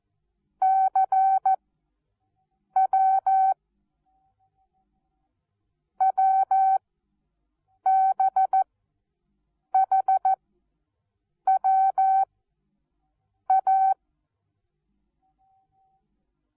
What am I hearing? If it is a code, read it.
Morse code, heard as a keyed tone, about 14 wpm, 770 Hz. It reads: C W W B H W A
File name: Onlinelehrgang / morsen / DJ4UF Morsen19